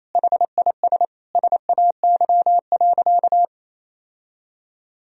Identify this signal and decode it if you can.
Morse code, heard as a keyed tone, about 28 wpm, 700 Hz. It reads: HIS SAY.
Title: Sentences from Top 1000 Words - Review Part 1 - 28wpm